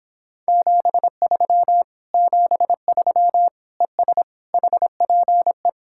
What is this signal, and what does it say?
Morse code, heard as a keyed tone, about 26 wpm, 700 Hz. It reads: 73 73 ES HPE